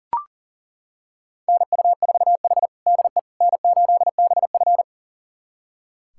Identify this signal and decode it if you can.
Morse code, heard as a keyed tone, about 40 wpm, 700 Hz. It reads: NU4H DE N8BF
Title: USA Callsign DE Callsign - 4 Characters - 40wpm